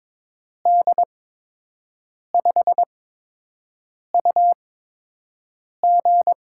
Morse code, heard as a keyed tone, about 22 wpm, spaced as for 6 wpm, 700 Hz. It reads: D5UG
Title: USA Call Signs - 6 Characters - 8x Character Spacing - 22wpm